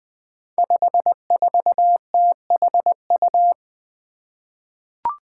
Morse code, heard as a keyed tone, about 20 wpm, 700 Hz. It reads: 54THU
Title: Five Challenging Letter-Number Combinations 20wpm